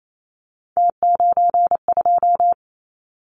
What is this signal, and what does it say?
Morse code, heard as a keyed tone, about 28 wpm, 700 Hz. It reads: T92
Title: Callsign Prefixes for Top 50 DXCC - 28wpm